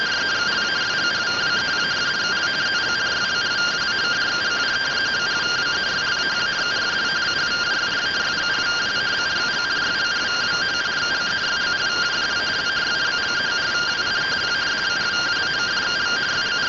Two-tone teleprinter signal, about 170 Hz shift.